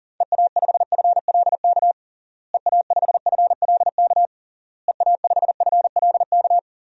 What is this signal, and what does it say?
Morse code, heard as a keyed tone, about 40 wpm, 700 Hz. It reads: EA5FLK EA5FLK EA5FLK
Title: DXCC-100 Call Signs - 6 Characters - Repeated 3x - 40wpm